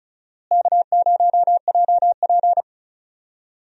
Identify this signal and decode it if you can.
Morse code, heard as a keyed tone, about 34 wpm, 700 Hz. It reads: K0JP
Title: USA Call Signs - 4 Characters - Repeated 3x - 35wpm